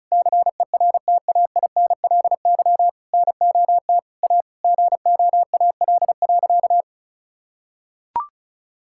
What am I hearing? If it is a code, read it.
Morse code, heard as a keyed tone, about 34 wpm, 700 Hz. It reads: CERTAINLY NOT A GOAL.
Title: Sentences from Top 1000 Words - Review Part 5 - 35wpm